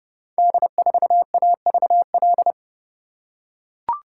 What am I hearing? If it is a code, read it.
Morse code, heard as a keyed tone, about 30 wpm, 700 Hz. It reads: D4AVL
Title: Code Groups - 5 Letter:Numbers - 30wpm